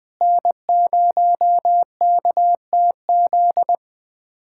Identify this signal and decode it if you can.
Morse code, heard as a keyed tone, about 20 wpm, 700 Hz. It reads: N0KTZ
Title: USA Callsign DE Callsign - 5 Characters - 20wpm